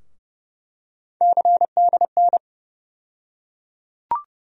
Morse code, heard as a keyed tone, about 30 wpm, 700 Hz. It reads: CDN
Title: USA Call Sign Postfixes - 3 Letters 30wpm